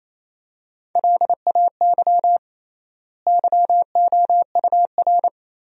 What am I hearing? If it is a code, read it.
Morse code, heard as a keyed tone, about 28 wpm, 700 Hz. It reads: LAY YOUR
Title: 3x Word-Spacing - Sentences from Top 400 Words - 28wpm